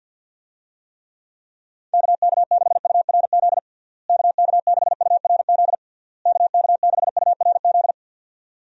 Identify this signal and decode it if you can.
Morse code, heard as a keyed tone, about 47 wpm, 700 Hz. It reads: KK6URZ KK6URZ KK6URZ
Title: USA Call Signs - 6 Characters - Repeated 3x - 50wpm